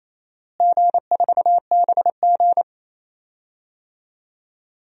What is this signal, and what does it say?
Morse code, heard as a keyed tone, about 28 wpm, 700 Hz. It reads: G4BG
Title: DXCC-100 Call Signs - 4 Characters - Repeated 2x - 28wpm